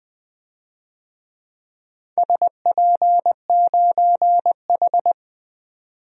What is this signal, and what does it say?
Morse code, heard as a keyed tone, about 20 wpm, 700 Hz. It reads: SP9H
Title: DXCC-100 Call Signs - 4 Characters - 20wpm